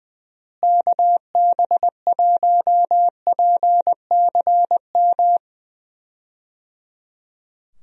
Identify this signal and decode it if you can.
Morse code, heard as a keyed tone, about 21 wpm, 700 Hz. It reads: KB1PCM
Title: USA Call Signs - 6 Characters - Repeated 3x - 20wpm